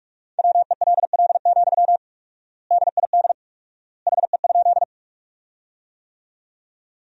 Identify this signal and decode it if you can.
Morse code, heard as a keyed tone, about 46 wpm, 700 Hz. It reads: WELL, DID HE?
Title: 4x Word-Spacing - Sentences from Top 200 Words - 45wpm